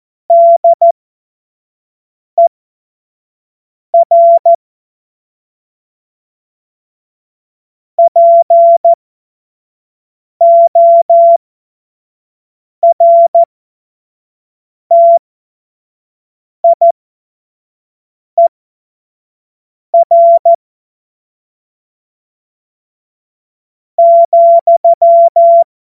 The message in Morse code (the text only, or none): DER PORTIER,